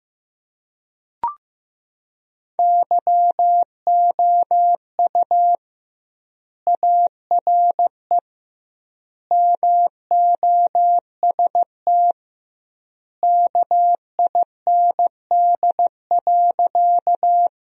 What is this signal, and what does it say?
Morse code, heard as a keyed tone, about 16 wpm, 700 Hz. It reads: YOU ARE MOST KIND.
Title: 2x Word-Spacing - Sentences from Top 200 Words - 15wpm